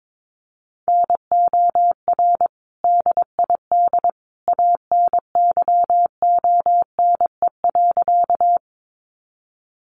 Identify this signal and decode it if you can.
Morse code, heard as a keyed tone, about 22 wpm, 700 Hz. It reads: NOR DID ANYONE.
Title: Sentences from Top 1500 Words 22wpm